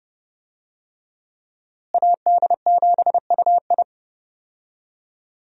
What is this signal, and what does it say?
Morse code, heard as a keyed tone, about 32 wpm, 700 Hz. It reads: AD7UI
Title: USA Call Signs - 5 Characters - Repeated 3x - 30wpm